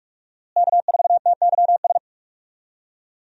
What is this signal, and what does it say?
Morse code, heard as a keyed tone, about 41 wpm, 700 Hz. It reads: K4TYS
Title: USA Call Signs - 5 Characters - Repeated 2x - 45wpm